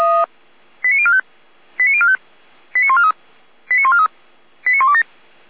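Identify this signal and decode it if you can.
Five-tone selective call (ZVEI), heard as quick runs of five tones, tones of about 70 ms. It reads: 89045 89045 89143 89143 89127
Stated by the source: Five-tone ZVEI selective call 89050, received on SDR.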